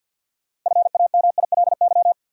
Morse code, heard as a keyed tone, about 53 wpm, 700 Hz. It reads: FAMILY
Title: Sentences from Top 400 Words - New Word Drill - 50wpm